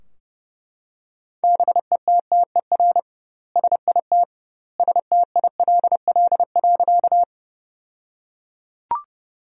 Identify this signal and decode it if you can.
Morse code, heard as a keyed tone, about 30 wpm, 700 Hz. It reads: BETTER SIT STILL.
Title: 2x Word-Spacing - Sentences from Top 500 Words - 30wpm